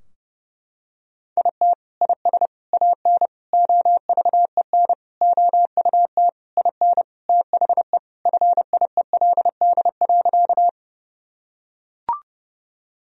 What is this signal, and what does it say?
Morse code, heard as a keyed tone, about 30 wpm, 700 Hz. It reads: IT IS AN OVEN OUT IN THE FIELD.
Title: Sentences from Top 1000 Words - Review Part 4 - 30wpm